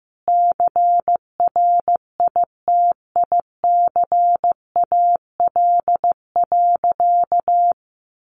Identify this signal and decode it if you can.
Morse code, heard as a keyed tone, about 15 wpm, 700 Hz. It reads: CRITICAL.
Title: Sentences from Top 1100 Words 15wpm